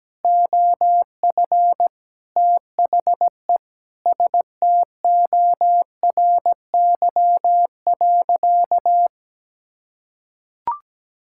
Morse code, heard as a keyed tone, about 17 wpm, 700 Hz. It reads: OF THE STORY.
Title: Sentences from Top 1000 Words - Review Part 6 - 17wpm